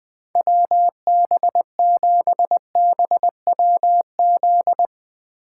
Morse code, heard as a keyed tone, about 20 wpm, 700 Hz. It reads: WB7BWZ
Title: 2023 SOTA Chaser Call Signs - 20wpm